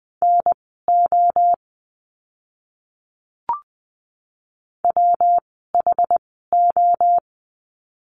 Morse code, heard as a keyed tone, about 20 wpm, spaced as for 14 wpm, 700 Hz. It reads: NO WHO
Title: 160 Lesson 26f - Letter g - Words 20wpm